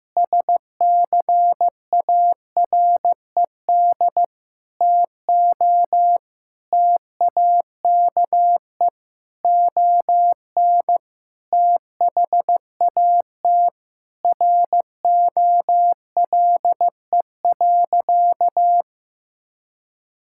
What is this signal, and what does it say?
Morse code, heard as a keyed tone, about 15 wpm, 700 Hz. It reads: SCARED TO TAKE ON THAT ROLE.
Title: Sentences from Top 1000 Words - Review Part 6 - 15wpm